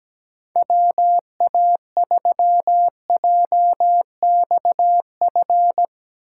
Morse code, heard as a keyed tone, about 17 wpm, 700 Hz. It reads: WA3JXF
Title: USA Call Signs - 6 Characters - 6x Character Spacing - 17wpm